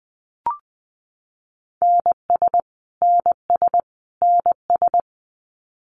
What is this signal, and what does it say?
Morse code, heard as a keyed tone, about 20 wpm, 700 Hz. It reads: NS NS NS
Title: Canadian Provinces and Territories - Repeated 3x - 20wpm